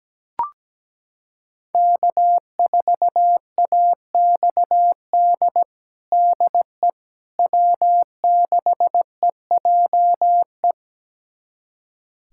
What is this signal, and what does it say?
Morse code, heard as a keyed tone, about 17 wpm, 700 Hz. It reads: K4AXD DE W6EJE
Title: USA Callsign DE Callsign 17wpm